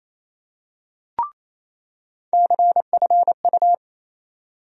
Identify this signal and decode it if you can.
Morse code, heard as a keyed tone, about 28 wpm, 700 Hz. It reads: CFU
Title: USA Call Sign Postfixes - 3 Letters 28wpm